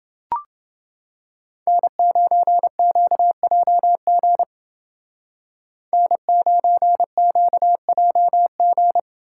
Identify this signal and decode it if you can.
Morse code, heard as a keyed tone, about 29 wpm, 700 Hz. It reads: N9QJG N9QJG